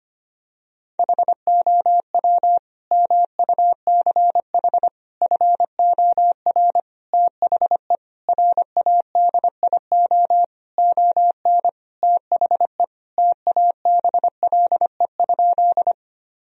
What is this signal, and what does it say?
Morse code, heard as a keyed tone, about 25 wpm, 700 Hz. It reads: HOW MUCH FOR THE RADIO ON THE TABLE?
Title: Sentences from Top 1000 Words - Review Part 3 - 25wpm